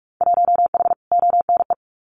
Morse code, heard as a keyed tone, about 44 wpm, 700 Hz. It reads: 1H ONE
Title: Winter Field Day Exchanges Encore 45wpm